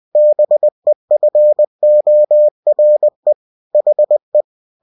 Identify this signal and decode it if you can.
Morse code, heard as a keyed tone, about 20 wpm, 600 Hz. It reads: BEFORE HE
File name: cw-books / winnie-the-pooh / winnie-20wpm-page-0072